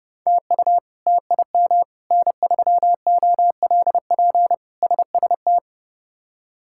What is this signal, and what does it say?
Morse code, heard as a keyed tone, about 31 wpm, 700 Hz. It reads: TU TIM N3OLP SST
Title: K1USN SST QSO Practice - 30wpm